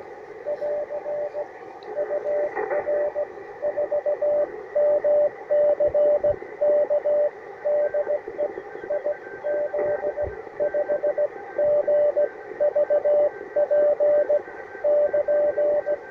Morse code, heard as a keyed tone, about 19 wpm, spaced as for 15 wpm, 600 Hz, with radio noise. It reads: <AR> UR4MCK DE IZ5GVP <KN>
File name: attachments / UR4MCK IZ5GVP